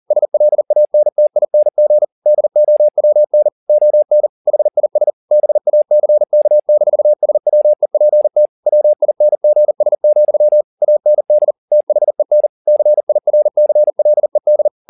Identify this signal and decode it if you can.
Morse code, heard as a keyed tone, about 40 wpm, 600 Hz. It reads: SLANTING DOWN ON HIS BACK-SWEPT WINGS, AND THEN CIRCLED